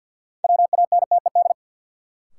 Morse code, heard as a keyed tone, about 53 wpm, 700 Hz. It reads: WANTED
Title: Sets of 2 Words - New Word Drill - 50wpm